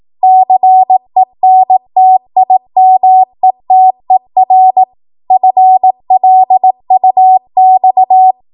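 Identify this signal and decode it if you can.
Morse code, heard as a keyed tone, about 18 wpm, 750 Hz. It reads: CENTIMETER FLUX
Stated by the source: ARRL 18 wpm Morse code practice archive transmission.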